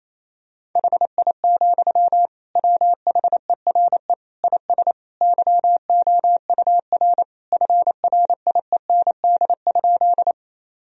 Morse code, heard as a keyed tone, about 27 wpm, 700 Hz. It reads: HI, WHERE IS YOUR FRIEND?
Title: Sentences from Top 1000 Words - Review Part 3 - 28wpm